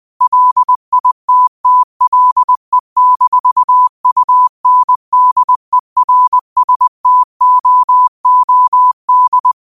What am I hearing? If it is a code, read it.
Morse code, heard as a keyed tone, about 20 wpm, 1000 Hz. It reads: LITTLE-UNDERSTOOD
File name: qotdfiles / 20wpm